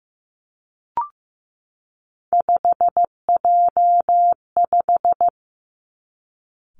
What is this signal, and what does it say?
Morse code, heard as a keyed tone, about 15 wpm, 700 Hz. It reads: 5J5
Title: Three Challenging Letter-Number Combinations 15wpm